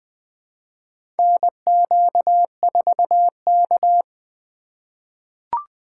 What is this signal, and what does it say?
Morse code, heard as a keyed tone, about 20 wpm, 700 Hz. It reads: NQ4K
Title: USA Call Signs - 4 Characters - 6x Character Spacing - 20wpm